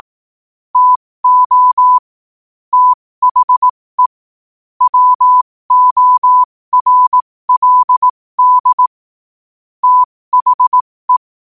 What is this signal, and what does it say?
Morse code, heard as a keyed tone, about 18 wpm, spaced as for 15 wpm, 1000 Hz. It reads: TO THE WORLD THE
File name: qotdfiles / 15wpm